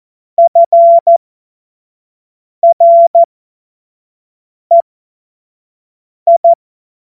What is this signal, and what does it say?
Morse code, heard as a keyed tone, about 14 wpm, spaced as for 5 wpm, 680 Hz. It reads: FREI
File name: Morsekurs / Witze / Witz 22. November 2022 25 F70